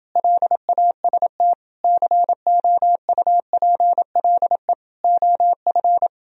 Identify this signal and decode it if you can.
Morse code, heard as a keyed tone, about 29 wpm, 700 Hz. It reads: LAST COUPLE OF